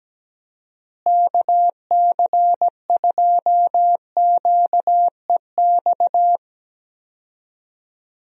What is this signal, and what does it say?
Morse code, heard as a keyed tone, about 17 wpm, 700 Hz. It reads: KC2QEX
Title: USA Call Signs - 6 Characters - Repeated 2x - 17wpm